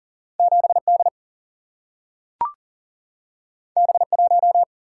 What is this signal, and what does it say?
Morse code, heard as a keyed tone, about 42 wpm, 700 Hz. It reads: 7 D B 1
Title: Single-Letter Single-Number 40wpm